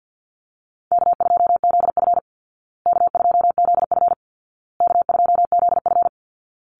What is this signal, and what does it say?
Morse code, heard as a keyed tone, about 50 wpm, 700 Hz. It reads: K2ZF K2ZF K2ZF
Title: Easy Call Signs Repeated 3x - 50wpm